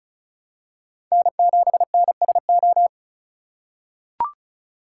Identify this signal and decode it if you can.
Morse code, heard as a keyed tone, about 36 wpm, 700 Hz. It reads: N7NSO
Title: USA Call Signs - 5 Characters - 4x Character Spacing - 35wpm